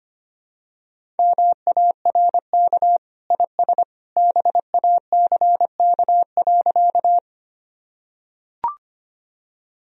Morse code, heard as a keyed tone, about 25 wpm, 700 Hz. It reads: MARK IS BACK.
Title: Sentences from Top 1000 Words - Review Part 1 - 25wpm